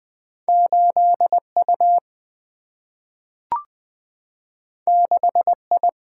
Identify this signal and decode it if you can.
Morse code, heard as a keyed tone, about 20 wpm, 700 Hz. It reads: 8U 6I
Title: Single-Letter Single-Number 20wpm